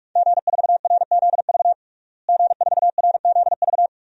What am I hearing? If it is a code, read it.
Morse code, heard as a keyed tone, about 46 wpm, 700 Hz. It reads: G4RZV G4RZV
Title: DXCC-100 Call Signs - 5 Characters - Repeated 3x - 45wpm